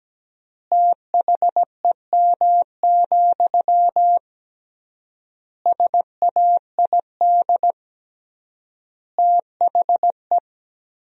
Morse code, heard as a keyed tone, about 17 wpm, 700 Hz. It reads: THEM, SAID THE
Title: 3x Word-Spacing - Sentences from Top 400 Words - 17wpm